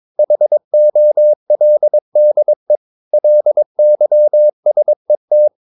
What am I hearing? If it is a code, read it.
Morse code, heard as a keyed tone, about 22 wpm, 600 Hz. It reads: HOLDE LYSET
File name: Ferie QRS 22wpm